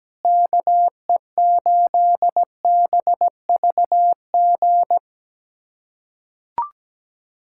Morse code, heard as a keyed tone, about 17 wpm, 700 Hz. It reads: KE8BVG
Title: USA Callsign DE Callsign - 6 Characters - 17wpm